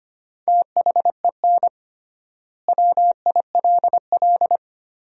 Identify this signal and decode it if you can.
Morse code, heard as a keyed tone, about 25 wpm, 700 Hz. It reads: THEN WILL
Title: 3x Word-Spacing - Sentences from Top 200 Words - 25wpm